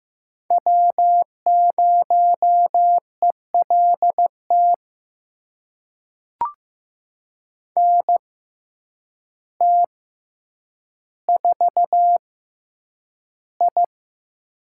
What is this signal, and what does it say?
Morse code, heard as a keyed tone, about 15 wpm, spaced as for 8 wpm, 700 Hz. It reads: W0ELT NT4I
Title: USA Call Signs - 6x Character Spacing - 15wpm